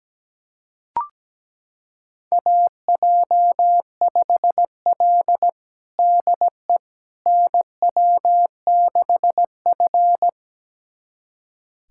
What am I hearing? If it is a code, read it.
Morse code, heard as a keyed tone, about 17 wpm, 700 Hz. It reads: AJ5L DE NW6F
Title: USA Callsign DE Callsign - 4 Characters - 17wpm